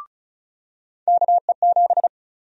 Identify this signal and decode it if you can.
Morse code, heard as a keyed tone, about 36 wpm, 700 Hz. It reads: KE7
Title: USA Call Sign Prefixes - 35wpm